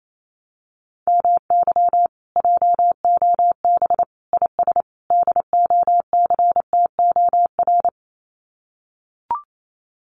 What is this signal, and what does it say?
Morse code, heard as a keyed tone, about 28 wpm, 700 Hz. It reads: MY JOB IS DOCTOR
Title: Common QSO Phrases Encore 28wpm